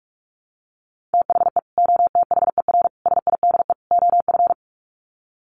Morse code, heard as a keyed tone, about 44 wpm, 700 Hz. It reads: THE OTHER SIDE OF
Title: Sets of 4 Words 45wpm